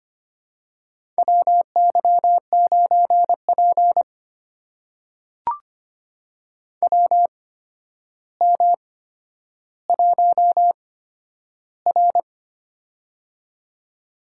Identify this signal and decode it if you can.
Morse code, heard as a keyed tone, about 25 wpm, spaced as for 11 wpm, 700 Hz. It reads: WY9P WM1R
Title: USA Call Signs - 4 Characters - 8x Character Spacing - 25wpm